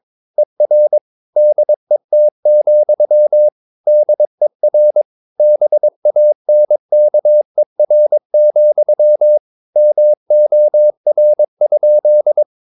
Morse code, heard as a keyed tone, about 22 wpm, 600 Hz. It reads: ER DET, DER BANKER, MOR?
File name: Ferie QRS 22wpm